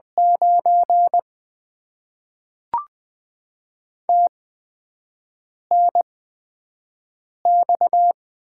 Morse code, heard as a keyed tone, about 20 wpm, spaced as for 5 wpm, 700 Hz. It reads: 9 TNX
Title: Top 90 QSO Elements - 8x Character Spacing - 20wpm